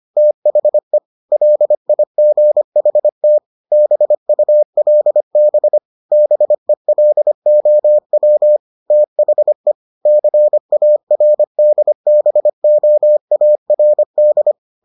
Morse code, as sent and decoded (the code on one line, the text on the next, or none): - .... / .-.. .. --. .... - / -... ..- .-.. -... / -... .-.. --- .-- / - .... / -.-. .- .-. -.. -... --- .- .-. -..
THE LIGHT BULB BELOW THE CARDBOARD